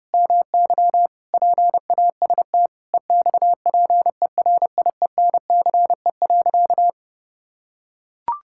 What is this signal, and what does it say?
Morse code, heard as a keyed tone, about 29 wpm, 700 Hz. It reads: MY PAST EXPERIENCE.
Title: Sentences from Top 1000 Words - Review Part 6 - 30wpm